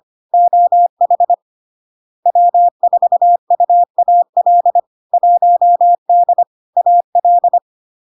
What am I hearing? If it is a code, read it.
Morse code, heard as a keyed tone, about 25 wpm, 700 Hz. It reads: OH W4UAL 1D AL
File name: LARK-FD-2022-Log-in-CW-25wpm-700-Hz